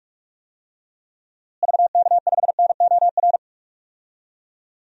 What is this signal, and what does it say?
Morse code, heard as a keyed tone, about 41 wpm, 700 Hz. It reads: VK5NOR
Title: DXCC-100 Call Signs - 6 Characters - Repeated 3x - 45wpm